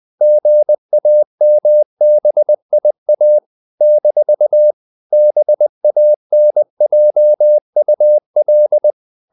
Morse code, heard as a keyed tone, about 20 wpm, 600 Hz. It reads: GAMBIA - BANJUL
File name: CW / Practice / NationalCapitals20